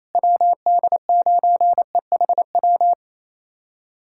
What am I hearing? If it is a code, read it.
Morse code, heard as a keyed tone, about 27 wpm, 700 Hz. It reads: WD9EHW
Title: USA Callsign DE Callsign 28wpm